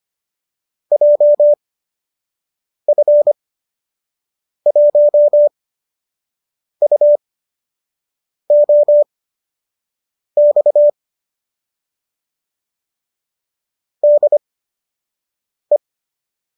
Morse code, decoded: JF1UOX DE

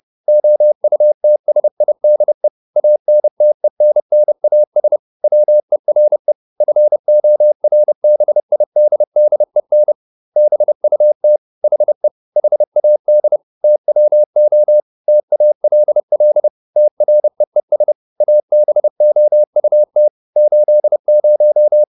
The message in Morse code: OUTSIDE ANTENNAS WERE FORBIDDEN BUT HE HAD TWO TALL TREES ABOUT 80

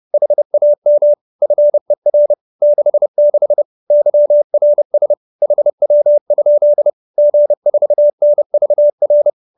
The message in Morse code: HAM FER 66 YRS HW? G4NVR

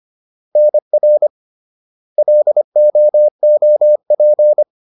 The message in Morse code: NR LOOP

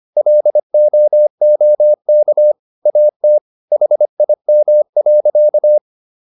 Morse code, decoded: LOOK AT HIM.

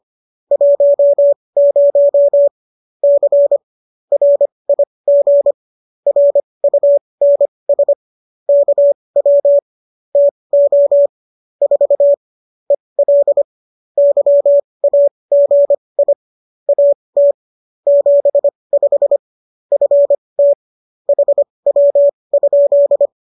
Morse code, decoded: 10 C RIG RUNS KW TO 4 EL YAGI AT 75 FT HW?

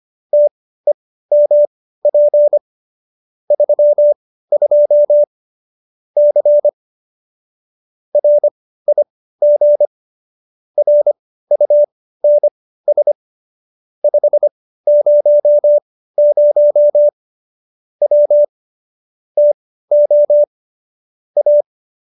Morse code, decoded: TEMP 32 C RIG RUNS 500 W TO A

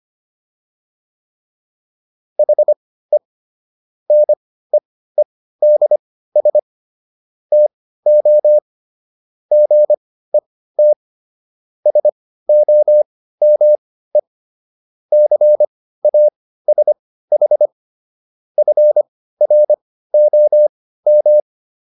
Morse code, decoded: HE NEEDS TO GET SOME CASH FROM

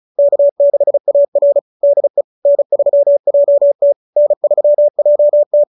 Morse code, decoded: K6AR DE N3JT N3JT

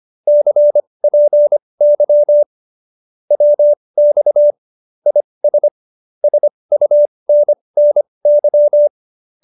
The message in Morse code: CPY WX IS SUNNY

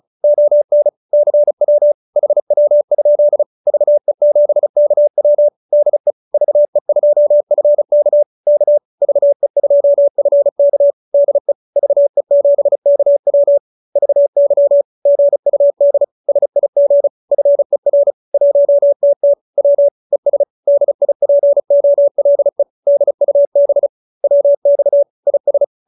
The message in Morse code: ON CW HW? VE7KW DE VE2FK K VE2FK DE VE7KW VY GUD SIG FER 1TT W ES DIPOLE DUB WX IS

